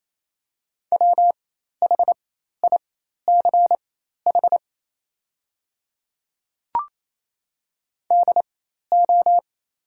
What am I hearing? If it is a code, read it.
Morse code, heard as a keyed tone, about 28 wpm, spaced as for 13 wpm, 700 Hz. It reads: WHICH DO
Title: 4x Character Spacing - Top 100 Words - 28wpm